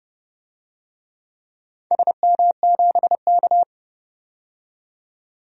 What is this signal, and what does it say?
Morse code, heard as a keyed tone, about 30 wpm, 700 Hz. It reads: SM7K
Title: DXCC-100 Call Signs - 4 Characters - 30wpm